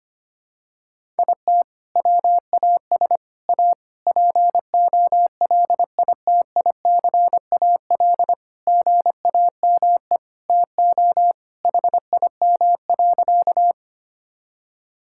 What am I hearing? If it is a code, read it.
Morse code, heard as a keyed tone, about 25 wpm, 700 Hz. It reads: IT WAS A POLITICAL GAME TO HIM.
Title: Sentences from Top 1000 Words - Review Part 5 - 25wpm